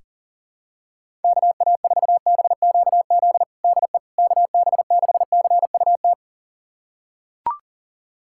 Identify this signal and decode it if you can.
Morse code, heard as a keyed tone, about 40 wpm, 700 Hz. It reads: KA4BQZ DE KB6CUT
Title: USA Callsign DE Callsign - 6 Characters - 40wpm